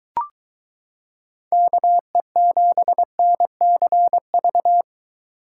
Morse code, heard as a keyed tone, about 23 wpm, 700 Hz. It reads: KE7NCV